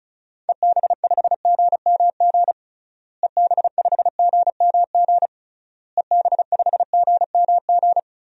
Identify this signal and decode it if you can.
Morse code, heard as a keyed tone, about 36 wpm, 700 Hz. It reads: EB5GMG EB5GMG EB5GMG
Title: DXCC-100 Call Signs - 6 Characters - Repeated 3x - 35wpm